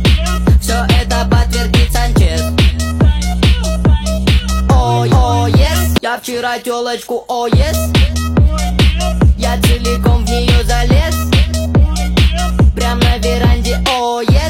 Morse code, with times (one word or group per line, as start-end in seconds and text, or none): none